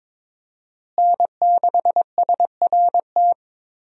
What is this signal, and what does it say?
Morse code, heard as a keyed tone, about 22 wpm, 700 Hz. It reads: N6SRT
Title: USA Call Signs - 5 Characters - Repeated 2x - 22wpm